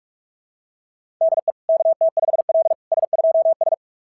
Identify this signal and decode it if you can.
Morse code, heard as a keyed tone, about 41 wpm, 660 Hz. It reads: DE KT5L S2S
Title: SOTA QSO Practice 45wpm